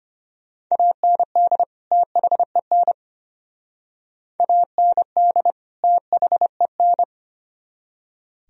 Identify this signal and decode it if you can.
Morse code, heard as a keyed tone, about 32 wpm, 700 Hz. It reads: AND THEN AND THEN